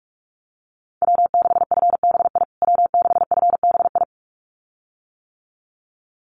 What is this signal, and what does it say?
Morse code, heard as a keyed tone, about 44 wpm, 690 Hz. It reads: W6FBI W6FBI
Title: USA Call Signs Repeated 2x 45wpm